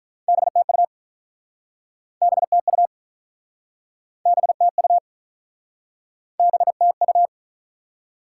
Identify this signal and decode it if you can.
Morse code, heard as a keyed tone, about 47 wpm, 700 Hz. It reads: BTU BTU BTU BTU